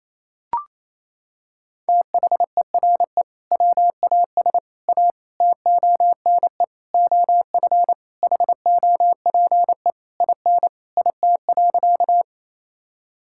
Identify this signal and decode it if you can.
Morse code, heard as a keyed tone, about 28 wpm, 700 Hz. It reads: THERE WAS A TONE OF HOPE IN IT.
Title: Sentences from Top 1600 Words 28wpm